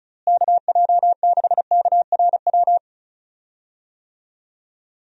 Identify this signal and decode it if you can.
Morse code, heard as a keyed tone, about 34 wpm, 700 Hz. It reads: KJ6KRW
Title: USA Call Signs - 6 Characters - Repeated 3x - 35wpm